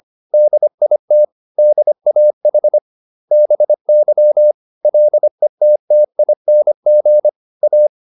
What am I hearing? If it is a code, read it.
Morse code, heard as a keyed tone, about 25 wpm, 600 Hz. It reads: DIT DAH BY LETTING A
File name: SS315 25